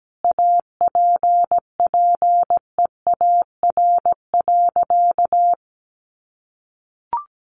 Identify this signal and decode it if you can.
Morse code, heard as a keyed tone, about 17 wpm, 700 Hz. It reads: APPEAR.
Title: Sentences from Top 1200 Words 17wpm